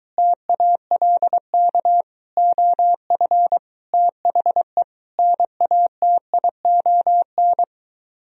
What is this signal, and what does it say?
Morse code, heard as a keyed tone, about 23 wpm, 700 Hz. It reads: TALK OF THE NATION